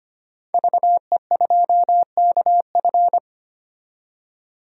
Morse code, heard as a keyed tone, about 25 wpm, 690 Hz. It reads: VE2KF
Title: DXCC-100 Call Signs - 5 Characters - 25wpm